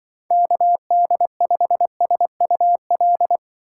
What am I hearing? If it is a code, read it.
Morse code, heard as a keyed tone, about 24 wpm, 700 Hz. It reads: KD5SUL
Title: Speed Racing - USA 2x3 Call Signs 20wpm